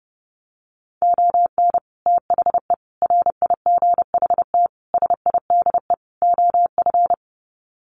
Morse code, heard as a keyed tone, about 30 wpm, 700 Hz. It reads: ON THE RIGHT SIDE OF
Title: Sets of 5 Words Encore 30wpm